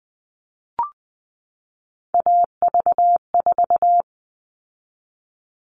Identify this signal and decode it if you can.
Morse code, heard as a keyed tone, about 20 wpm, 700 Hz. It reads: AV4
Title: All Cut Numbers - 3-Digit 20wpm